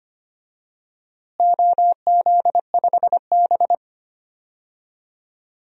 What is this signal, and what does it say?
Morse code, heard as a keyed tone, about 25 wpm, 700 Hz. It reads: OZ5B
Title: DXCC-100 Call Signs - 4 Characters - Repeated 3x - 25wpm